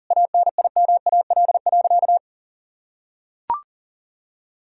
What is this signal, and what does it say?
Morse code, heard as a keyed tone, about 37 wpm, 700 Hz. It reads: ANIMAL.
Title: Sentences from Top 1000 Words - Review Part 1 - 40wpm